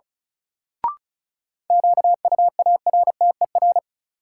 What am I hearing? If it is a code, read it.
Morse code, heard as a keyed tone, about 34 wpm, 700 Hz. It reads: QUARTER
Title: Sentences from Top 1300 Words - New Word Drill Alt - 35wpm